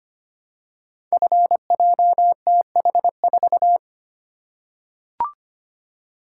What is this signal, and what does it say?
Morse code, heard as a keyed tone, about 25 wpm, 700 Hz. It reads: FJTH4
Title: Code Groups - 5 Letter:Numbers - 25wpm